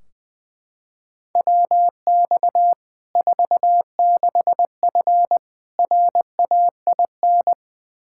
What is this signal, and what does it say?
Morse code, heard as a keyed tone, about 21 wpm, 700 Hz. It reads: WX 46F RAIN
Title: Practice QSO - Exchange 2 - Info - 20wpm